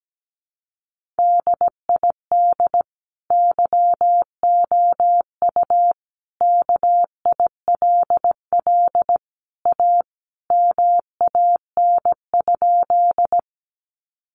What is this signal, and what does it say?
Morse code, heard as a keyed tone, about 17 wpm, 700 Hz. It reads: DID YOU KILL A MAN?
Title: Sentences from Top 1400 Words 17wpm